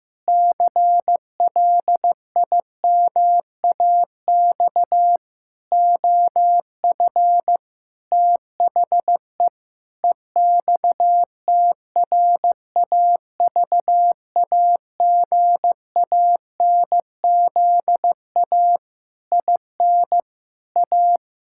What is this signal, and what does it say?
Morse code, heard as a keyed tone, about 15 wpm, 700 Hz. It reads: CLIMAX OF THE EXTRAVAGANZA IN A
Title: Pangram Challenge v2 - 15wpm